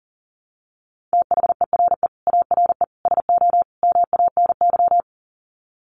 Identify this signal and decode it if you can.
Morse code, heard as a keyed tone, about 40 wpm, 700 Hz. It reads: THERE ARE SO MANY
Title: Sets of 4 Words 40wpm